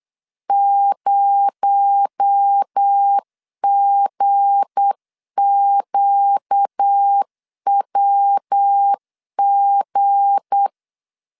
Morse code, heard as a keyed tone, about 8 wpm, 800 Hz. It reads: 0GQWG
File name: LEZIONI / 7 GF9Z / 7 G 8wpm 3118